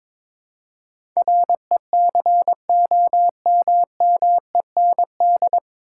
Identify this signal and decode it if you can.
Morse code, heard as a keyed tone, about 22 wpm, 700 Hz. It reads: RECOMMEND
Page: Morse Code Ninja practice 22wpm